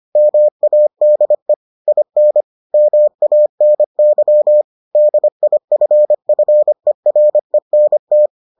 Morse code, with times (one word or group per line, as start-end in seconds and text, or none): 0.2-1.5 MADE
1.9-2.4 IN
2.7-4.6 MANY
5.0-8.3 DIFFERENT